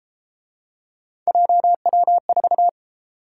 Morse code, heard as a keyed tone, about 31 wpm, 700 Hz. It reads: JW4